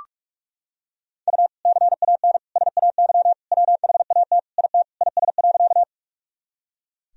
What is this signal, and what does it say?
Morse code, heard as a keyed tone, about 46 wpm, 700 Hz. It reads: U CAN SAY WHAT IT IS.
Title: Sentences from Top 200 Words and Abbreviations 45wpm